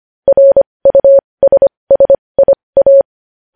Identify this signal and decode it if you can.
Morse code, heard as a keyed tone, about 25 wpm, spaced as for 20 wpm, 550 Hz. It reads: RUSSIA
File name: podcast / News headlines-1x-25x20